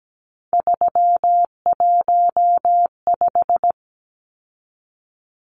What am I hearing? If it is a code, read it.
Morse code, heard as a keyed tone, about 17 wpm, 700 Hz. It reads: 315
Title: Numbers - 3-Digit 17wpm